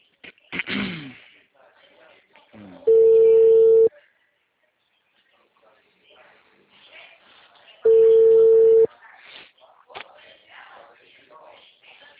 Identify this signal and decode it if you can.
Ringback tone (China, 450 Hz), with speech.